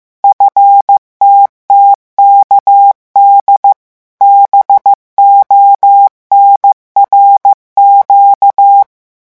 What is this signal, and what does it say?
Morse code, heard as a keyed tone, about 15 wpm, 800 Hz. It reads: FTTKD BONRQ
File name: morse mp3 / revision8-vitesse 14 mots